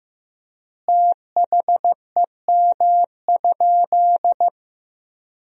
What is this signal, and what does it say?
Morse code, heard as a keyed tone, about 15 wpm, 700 Hz. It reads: THEM?
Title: 3x Word-Spacing - Sentences from Top 100 Words - 15wpm